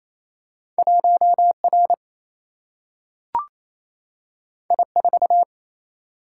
Morse code, heard as a keyed tone, about 28 wpm, 700 Hz. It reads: 1 R I 4
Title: Single-Letter Single-Number 28wpm